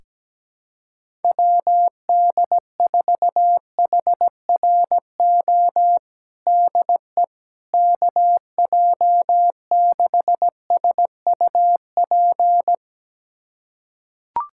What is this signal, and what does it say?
Morse code, heard as a keyed tone, about 17 wpm, 700 Hz. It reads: WD4HRO DE KJ6SUP
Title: USA Callsign DE Callsign 17wpm